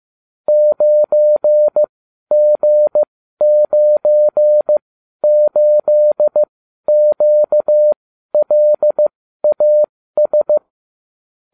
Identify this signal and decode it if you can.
Morse code, heard as a keyed tone, about 15 wpm, spaced as for 12 wpm, 600 Hz. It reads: 9G98QLAS